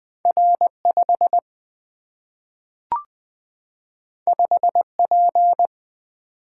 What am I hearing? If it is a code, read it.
Morse code, heard as a keyed tone, about 20 wpm, 700 Hz. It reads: R 5 5 P
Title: Single-Letter Single-Number 20wpm